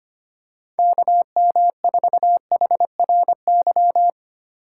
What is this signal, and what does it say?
Morse code, heard as a keyed tone, about 25 wpm, 700 Hz. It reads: KM4HRY
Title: USA Call Signs - 6 Characters - 4x Character Spacing - 25wpm